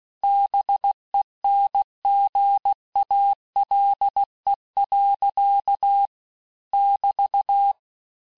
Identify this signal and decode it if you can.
Morse code, heard as a keyed tone, about 16 wpm, 800 Hz. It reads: BENGALE. <BT>
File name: teste-clair-vitesse-15mpm-puis-20mpm